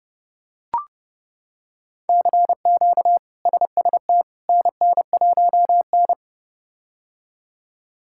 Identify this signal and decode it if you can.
Morse code, heard as a keyed tone, about 30 wpm, 700 Hz. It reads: CQ SST NN1N
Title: K1USN SST QSO Practice - 30wpm